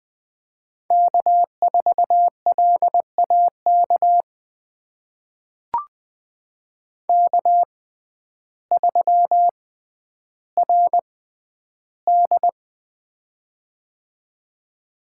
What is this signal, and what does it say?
Morse code, heard as a keyed tone, about 20 wpm, spaced as for 11 wpm, 700 Hz. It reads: K4LAK K3RD
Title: USA Call Signs - 6x Character Spacing - 20wpm